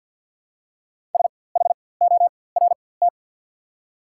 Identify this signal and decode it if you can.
Morse code, heard as a keyed tone, about 54 wpm, spaced as for 24 wpm, 690 Hz. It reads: SHORT
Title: 4x Character Spacing - Top 500 Words - 50wpm